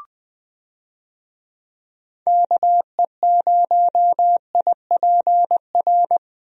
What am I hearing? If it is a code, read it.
Morse code, heard as a keyed tone, about 21 wpm, 700 Hz. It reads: KE0IPR
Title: USA Call Signs - 6 Characters - Repeated 3x - 20wpm